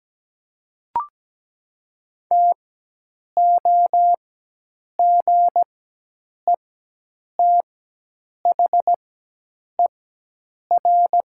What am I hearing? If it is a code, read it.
Morse code, heard as a keyed tone, about 17 wpm, spaced as for 8 wpm, 700 Hz. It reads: TOGETHER
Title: 4x Character Spacing - Top 300 Words - 17wpm